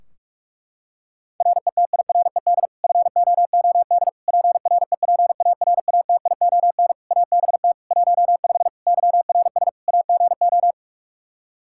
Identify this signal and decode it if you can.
Morse code, heard as a keyed tone, about 46 wpm, 700 Hz. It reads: RETIRED FOOD PREPARATION ABT 15 YRS AGO
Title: K1USN SST QSO Practice 45wpm